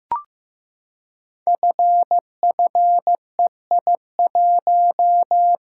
Morse code, heard as a keyed tone, about 15 wpm, 700 Hz. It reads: FFEI1
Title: Code Groups - 5 Letter:Numbers - 15wpm